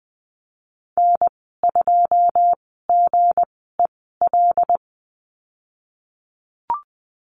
Morse code, heard as a keyed tone, about 20 wpm, spaced as for 15 wpm, 700 Hz. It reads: N2GEL
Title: 161 Lesson 26g - Letter g - Call Signs 20wpm